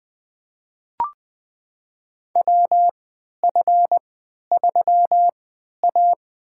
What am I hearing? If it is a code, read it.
Morse code, heard as a keyed tone, about 20 wpm, spaced as for 11 wpm, 700 Hz. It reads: WF3A
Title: USA Call Signs - 3x Character Spacing - 20wpm